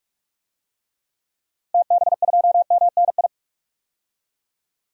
Morse code, heard as a keyed tone, about 41 wpm, 700 Hz. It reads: TB2MNI
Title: DXCC-100 Call Signs - 6 Characters - Repeated 2x - 45wpm